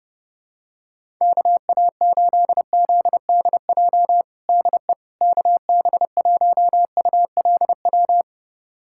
Morse code, heard as a keyed tone, about 30 wpm, 700 Hz. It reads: KA8ZDJ DE KB1ULW
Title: USA Callsign DE Callsign 30wpm